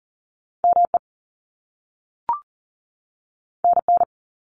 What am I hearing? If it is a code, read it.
Morse code, heard as a keyed tone, about 39 wpm, 700 Hz. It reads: ME NN
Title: Top Two Letter Combinations in Words 40wpm